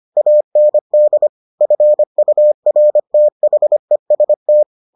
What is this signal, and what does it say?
Morse code, heard as a keyed tone, about 25 wpm, 600 Hz. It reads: AND FURTHEST